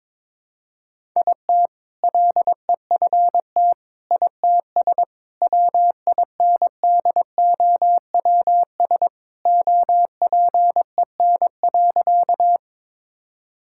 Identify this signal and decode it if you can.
Morse code, heard as a keyed tone, about 22 wpm, 700 Hz. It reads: IT LEFT ITS WINDOWS OPEN.
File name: All-Age Appropriate Jokes - 22wpm